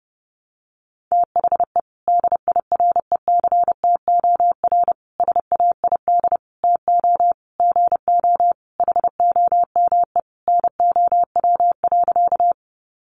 Morse code, heard as a keyed tone, about 30 wpm, 700 Hz. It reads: THE DIRECTOR SAID TO GO HOME NOW.
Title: Sentences from Top 1300 Words 30wpm